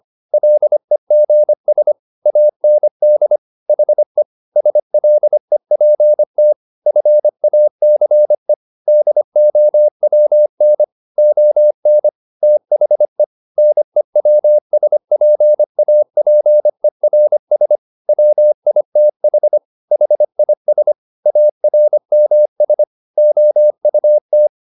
Morse code, heard as a keyed tone, about 25 wpm, 600 Hz. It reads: LEGS AND HE SLEPT FACE DOWN ON THE NEWSPAPERS WITH HIS ARMS OUT